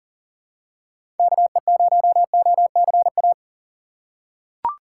Morse code, heard as a keyed tone, about 40 wpm, 700 Hz. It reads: KE0OCA
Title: USA Call Signs - 2x Character Spacing - 40wpm